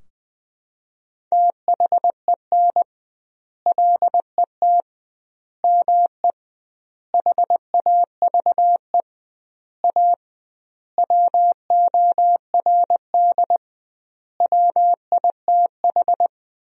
Morse code, heard as a keyed tone, about 20 wpm, 700 Hz. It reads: THEN LET ME HAVE A WORD WITH
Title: 2x Word-Spacing - Sentences from Top 300 Words - 20wpm